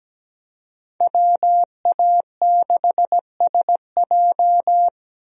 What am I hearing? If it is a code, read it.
Morse code, heard as a keyed tone, about 17 wpm, 700 Hz. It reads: WA6SJ
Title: USA Call Signs - 5 Characters - Repeated 2x - 17wpm